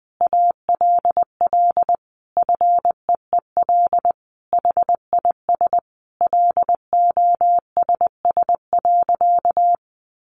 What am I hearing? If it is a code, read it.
Morse code, heard as a keyed tone, about 20 wpm, 700 Hz. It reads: ALL FEEL HIS LOSS.
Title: Sentences from Top 1400 Words 20wpm